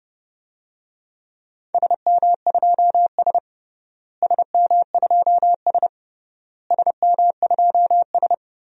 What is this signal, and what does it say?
Morse code, heard as a keyed tone, about 30 wpm, 700 Hz. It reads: SM2S SM2S SM2S
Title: DXCC-100 Call Signs - 4 Characters - Repeated 3x - 30wpm